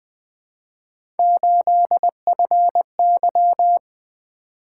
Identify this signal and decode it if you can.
Morse code, heard as a keyed tone, about 20 wpm, 700 Hz. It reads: 8FY
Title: Three Letter-Number Combinations - 20wpm